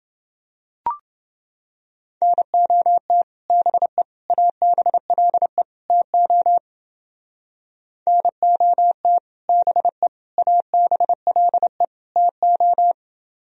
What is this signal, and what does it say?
Morse code, heard as a keyed tone, about 29 wpm, 700 Hz. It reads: NOT BE ABLE TO NOT BE ABLE TO